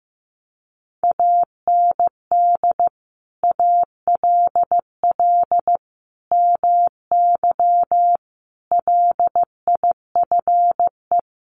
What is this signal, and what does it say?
Morse code, heard as a keyed tone, about 15 wpm, 700 Hz. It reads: AND ALL MY LIFE
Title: Sentences from Top 1500 Words 15wpm